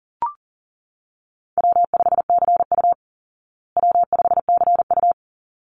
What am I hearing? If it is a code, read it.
Morse code, heard as a keyed tone, about 40 wpm, 700 Hz. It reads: W5CU W5CU
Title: Easy Call Signs Repeated 2x - 40wpm